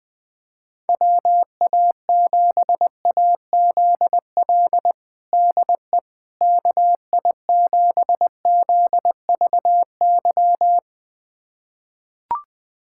Morse code, heard as a keyed tone, about 21 wpm, 700 Hz. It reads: WA7AZL DE KI7ZVY
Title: USA Callsign DE Callsign - 6 Characters - 20wpm